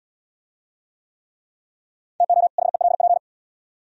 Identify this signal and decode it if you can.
Morse code, heard as a keyed tone, about 68 wpm, 700 Hz. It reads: TO HELP